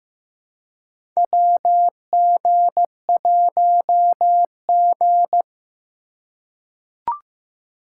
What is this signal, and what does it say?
Morse code, heard as a keyed tone, about 15 wpm, 700 Hz. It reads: WG1G
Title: USA Call Signs - 4 Characters - 3x Character Spacing - 15wpm